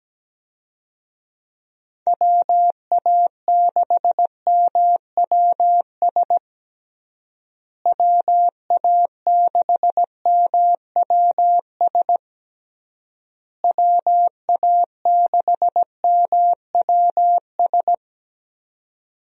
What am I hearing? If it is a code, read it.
Morse code, heard as a keyed tone, about 17 wpm, 700 Hz. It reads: WA6MWS WA6MWS WA6MWS
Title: USA Call Signs - 6 Characters - Repeated 3x - 17wpm